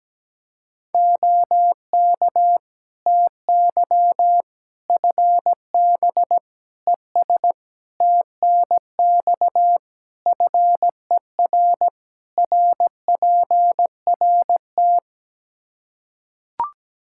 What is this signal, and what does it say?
Morse code, heard as a keyed tone, about 17 wpm, 700 Hz. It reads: OK TY FB ES TNX FER RPRT
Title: Practice QSO - Exchange 2 - Info - 17wpm